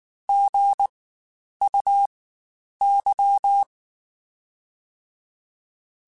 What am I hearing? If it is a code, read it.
Morse code, heard as a keyed tone, about 19 wpm, spaced as for 9 wpm, 800 Hz. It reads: GUY
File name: Spaced / Chuckles Spaced-mp3s / HumanErr sp @18